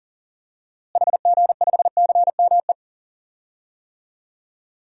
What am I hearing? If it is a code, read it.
Morse code, heard as a keyed tone, about 37 wpm, 700 Hz. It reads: HG5CME
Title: DXCC-100 Call Signs - 6 Characters - Repeated 3x - 40wpm